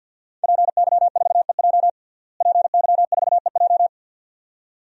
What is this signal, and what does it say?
Morse code, heard as a keyed tone, about 52 wpm, 700 Hz. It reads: PY4EJ PY4EJ
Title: DXCC-100 Call Signs - 5 Characters - Repeated 2x - 50wpm